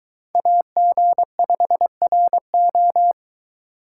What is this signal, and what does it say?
Morse code, heard as a keyed tone, about 22 wpm, 700 Hz. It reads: AG5RO